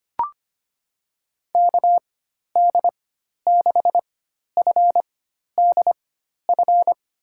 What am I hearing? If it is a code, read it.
Morse code, heard as a keyed tone, about 25 wpm, spaced as for 12 wpm, 700 Hz. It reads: KD6FDF
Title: USA Call Signs - 6 Characters - 4x Character Spacing - 25wpm